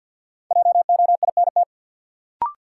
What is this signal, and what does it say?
Morse code, heard as a keyed tone, about 46 wpm, 700 Hz. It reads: JOINT
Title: Sentences from Top 1300 Words - New Word Drill Alt - 50wpm